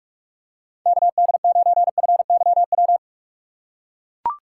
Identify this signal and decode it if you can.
Morse code, heard as a keyed tone, about 43 wpm, 700 Hz. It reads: KD9FYW
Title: USA Call Signs - 2x Character Spacing - 45wpm